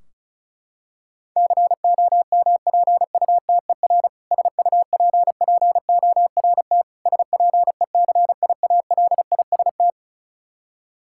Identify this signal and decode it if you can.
Morse code, heard as a keyed tone, about 34 wpm, 700 Hz. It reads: COMPUTER SUPPORT SPECIALIST
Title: Practice QSO - Exchange 3 - More Info - 35wpm